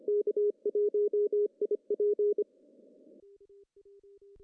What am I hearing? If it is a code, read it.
Morse code, heard as a keyed tone, about 25 wpm, 410 Hz. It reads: K1IP
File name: CW / cqww-cw-k5tr